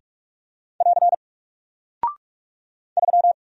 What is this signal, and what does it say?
Morse code, heard as a keyed tone, about 44 wpm, 700 Hz. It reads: <AR> 3
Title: Single Character (Letter:Number:Punctuation:Prosign) 45wpm